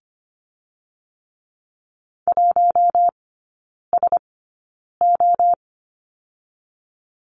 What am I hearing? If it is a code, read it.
Morse code, heard as a keyed tone, about 25 wpm, spaced as for 9 wpm, 700 Hz. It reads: 1SO
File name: Lesson 15e - Number 5 - Three Characters 25wpm